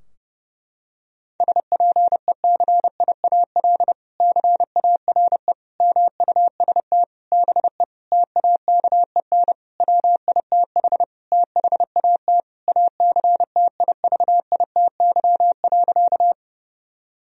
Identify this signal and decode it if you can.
Morse code, heard as a keyed tone, about 30 wpm, 700 Hz. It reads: SPECIAL CARE MUST BE TAKEN WITH THAT ACTIVITY.
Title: Sentences from Top 1000 Words - Review Part 4 - 30wpm